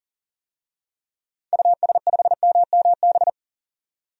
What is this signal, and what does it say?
Morse code, heard as a keyed tone, about 40 wpm, 700 Hz. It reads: US5MMB
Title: DXCC-100 Call Signs - 6 Characters - 40wpm